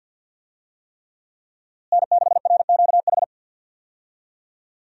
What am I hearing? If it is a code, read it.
Morse code, heard as a keyed tone, about 52 wpm, 700 Hz. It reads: N6RQH
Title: USA Call Signs - 5 Characters - Repeated 3x - 50wpm